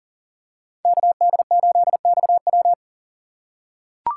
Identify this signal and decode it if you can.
Morse code, heard as a keyed tone, about 40 wpm, 700 Hz. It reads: KD8XW
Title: USA Call Signs - 5 Characters - 2x Character Spacing - 40wpm